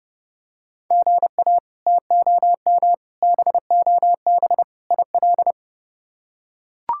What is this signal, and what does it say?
Morse code, heard as a keyed tone, about 30 wpm, 700 Hz. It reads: GA TOM BOB IL
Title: K1USN SST QSO Practice - 30wpm